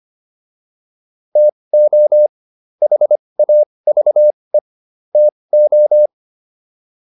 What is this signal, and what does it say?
Morse code, heard as a keyed tone, about 25 wpm, spaced as for 20 wpm, 600 Hz. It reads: TO HAVE TO